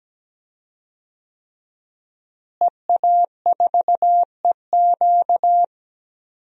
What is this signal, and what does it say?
Morse code, heard as a keyed tone, about 17 wpm, 700 Hz. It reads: EA4EQ
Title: DXCC-100 Call Signs - 5 Characters - Repeated 3x - 17wpm